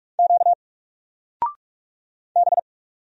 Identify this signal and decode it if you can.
Morse code, heard as a keyed tone, about 45 wpm, 700 Hz. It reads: Q B
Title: Single Letter-Number 45wpm